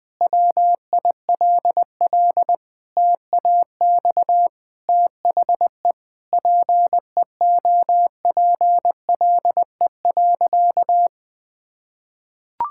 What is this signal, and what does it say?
Morse code, heard as a keyed tone, about 20 wpm, 700 Hz. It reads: WILL TAX THE PEOPLE.
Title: Sentences from Top 1000 Words - Review Part 6 - 20wpm